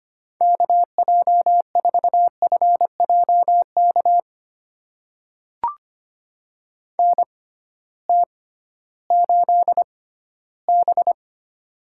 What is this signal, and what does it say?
Morse code, heard as a keyed tone, about 25 wpm, spaced as for 14 wpm, 700 Hz. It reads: KJ4FJK NT8B
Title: USA Call Signs - 6x Character Spacing - 25wpm